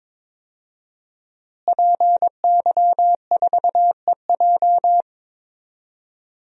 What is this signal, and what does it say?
Morse code, heard as a keyed tone, about 21 wpm, 700 Hz. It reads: PY4EJ
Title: DXCC-100 Call Signs - 5 Characters - Repeated 2x - 22wpm